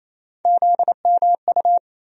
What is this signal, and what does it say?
Morse code, heard as a keyed tone, about 28 wpm, 700 Hz. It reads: ZMU
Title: USA Call Sign Postfixes - 3 Letters 28wpm